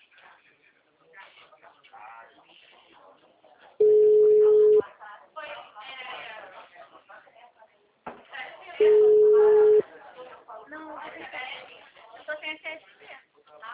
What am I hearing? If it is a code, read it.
Ringback tone (Europe, 425 Hz), with speech.